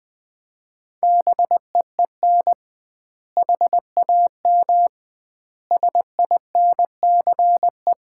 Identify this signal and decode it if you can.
Morse code, heard as a keyed tone, about 21 wpm, 700 Hz. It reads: BEEN HAM SINCE
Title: Common QSO Phrases - 2x Word-Spacing - 20wpm